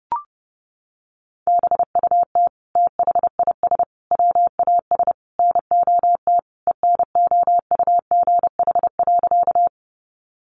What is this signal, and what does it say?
Morse code, heard as a keyed tone, about 30 wpm, 700 Hz. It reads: BUT THIS WAS NOT ENOUGH.
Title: Sentences from Top 600 Words 30wpm